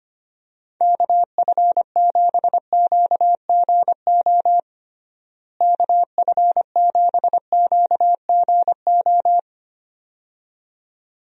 Morse code, heard as a keyed tone, about 25 wpm, 700 Hz. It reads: KF7QGO KF7QGO
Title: USA Call Signs - 6 Characters - Repeated 2x - 25wpm